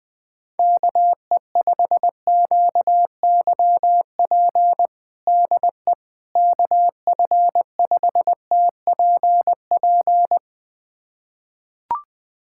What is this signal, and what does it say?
Morse code, heard as a keyed tone, about 20 wpm, 700 Hz. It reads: KE5QYP DE KF5TPP
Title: USA Callsign DE Callsign - 6 Characters - 20wpm